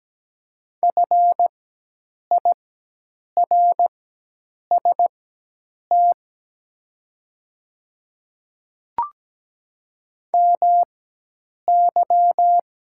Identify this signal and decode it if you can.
Morse code, heard as a keyed tone, about 17 wpm, spaced as for 8 wpm, 700 Hz. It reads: FIRST MY
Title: 4x Character Spacing - Top 300 Words - 17wpm